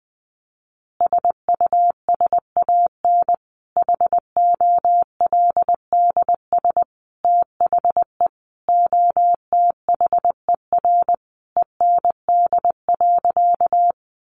Morse code, heard as a keyed tone, about 20 wpm, 700 Hz. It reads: SUSAN HOLDS THE OTHER END.
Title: Easy Phrases 20wpm